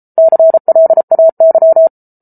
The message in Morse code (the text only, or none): CLAY